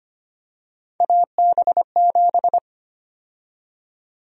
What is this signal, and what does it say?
Morse code, heard as a keyed tone, about 25 wpm, 700 Hz. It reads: AB7
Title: USA Call Sign Prefixes - 25wpm